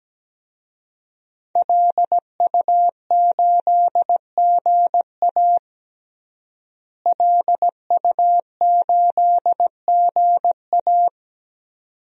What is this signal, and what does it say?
Morse code, heard as a keyed tone, about 17 wpm, 700 Hz. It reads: LU8GA LU8GA
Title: DXCC-100 Call Signs - 5 Characters - Repeated 2x - 17wpm